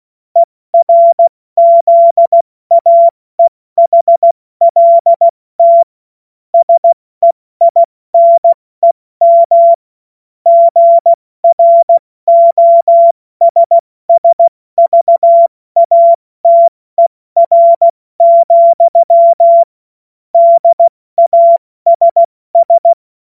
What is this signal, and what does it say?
Morse code, heard as a keyed tone, about 16 wpm, spaced as for 14 wpm, 680 Hz. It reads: ERZAEHLT SEINEM GROSSVATER, DASS
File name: Morsekurs / Witze / Witz 19. Oktober 2021, 70 F80